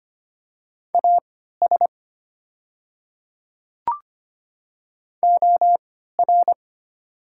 Morse code, heard as a keyed tone, about 25 wpm, spaced as for 14 wpm, 700 Hz. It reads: AS OR
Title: 3x Character Spacing - Top 100 Words - 25wpm